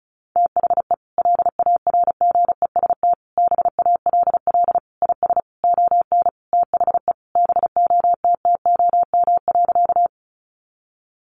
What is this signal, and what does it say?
Morse code, heard as a keyed tone, about 34 wpm, 700 Hz. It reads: THE LARGEST BALL IS ON THE BOTTOM.
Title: Easy Phrases 35wpm